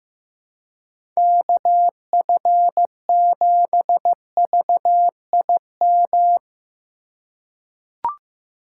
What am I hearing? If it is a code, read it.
Morse code, heard as a keyed tone, about 15 wpm, 700 Hz. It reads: KF7VIM
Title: USA Call Signs - 6 Characters - 4x Character Spacing - 15wpm